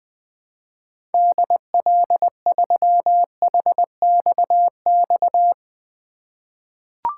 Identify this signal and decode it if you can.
Morse code, heard as a keyed tone, about 20 wpm, 700 Hz. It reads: DL3HXX
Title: 2023 SOTA Chaser Call Signs - 20wpm